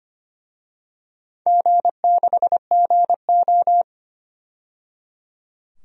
Morse code, heard as a keyed tone, about 25 wpm, 700 Hz. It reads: G6GO
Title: DXCC-100 Call Signs - 4 Characters - 25wpm